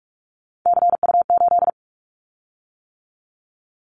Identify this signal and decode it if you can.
Morse code, heard as a keyed tone, about 46 wpm, 700 Hz. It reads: CU8
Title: Callsign Prefixes for Top 50 DXCC - 45wpm